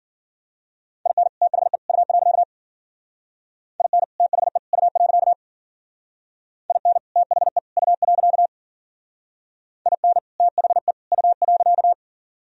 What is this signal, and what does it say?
Morse code, heard as a keyed tone, about 50 wpm, 700 Hz. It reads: IN THE U. IN THE U. IN THE U. IN THE U.
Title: Speed-Racing - Sets of 3 Words 40wpm